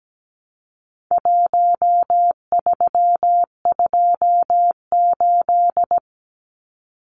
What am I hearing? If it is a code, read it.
Morse code, heard as a keyed tone, about 17 wpm, 700 Hz. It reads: 1328
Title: Numbers - 4-Digit 17wpm